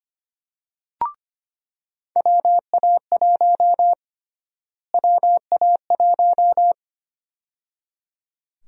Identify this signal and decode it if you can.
Morse code, heard as a keyed tone, about 25 wpm, 700 Hz. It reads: WA1 WA1
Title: USA Call Sign Prefixes - Repeated 2x - 25wpm